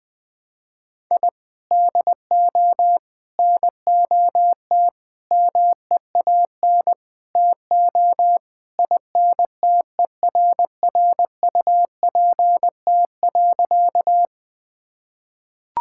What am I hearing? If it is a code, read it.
Morse code, heard as a keyed tone, about 21 wpm, 700 Hz. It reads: I DO NOT MEAN TO INTERRUPT.
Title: Sentences with 5-Word Phrases - 20wpm